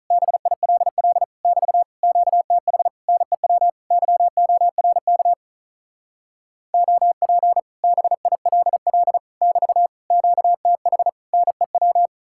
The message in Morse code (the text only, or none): BILL <BT> QTH NEW YORK OP BILL <BT> QTH NEW